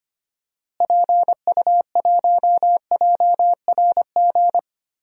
Morse code, heard as a keyed tone, about 25 wpm, 700 Hz. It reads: PU1JRG
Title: DXCC-100 Call Signs - 6 Characters - Repeated 2x - 25wpm